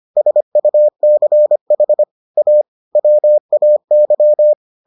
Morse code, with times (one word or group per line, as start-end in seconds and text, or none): 0.2-2.0 SUCH
2.4-2.6 A
3.0-4.5 WAY